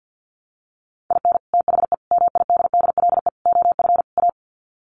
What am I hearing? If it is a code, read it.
Morse code, heard as a keyed tone, about 49 wpm, 700 Hz. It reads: IN THE MIDDLE OF A
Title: Sets of 5 Words 50wpm